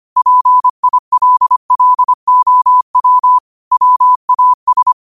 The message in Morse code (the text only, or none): PILLOW WAS